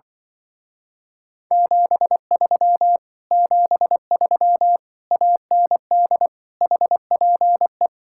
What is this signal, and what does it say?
Morse code, heard as a keyed tone, about 24 wpm, 700 Hz. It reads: 73 73 AND HPE
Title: Speed-Racing Common QSO Phrases - 20wpm